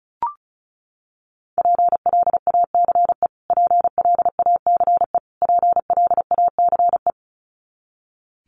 Morse code, heard as a keyed tone, about 34 wpm, 700 Hz. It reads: PLACE PLACE PLACE
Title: Top 500 Words Repeated 35wpm